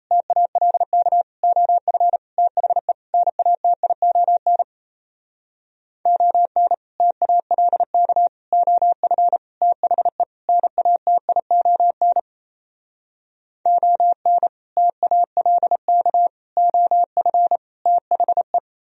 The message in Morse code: TALK OF THE NATION ON TALK OF THE NATION ON TALK OF THE

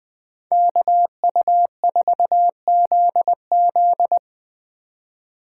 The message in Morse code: KU4ZZ

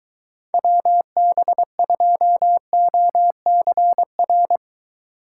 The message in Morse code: WB2OCR

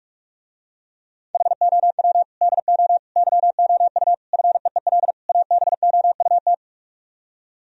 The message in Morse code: HOW DO YOU FEEL ABOUT

